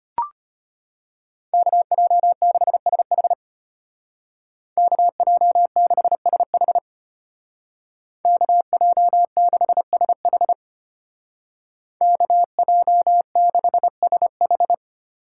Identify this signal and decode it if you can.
Morse code, heard as a keyed tone, about 31 wpm, 700 Hz. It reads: KJ6SH KJ6SH KJ6SH KJ6SH
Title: Speed Racing - USA 2x2 Call Signs 25wpm